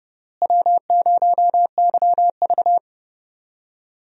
Morse code, heard as a keyed tone, about 31 wpm, 700 Hz. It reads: W0YV
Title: USA Call Signs - 4 Characters - Repeated 2x - 30wpm